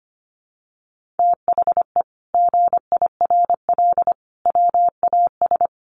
Morse code, heard as a keyed tone, about 25 wpm, 700 Hz. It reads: THE GIRL WAS
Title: Sentences from Top 1000 Words 25wpm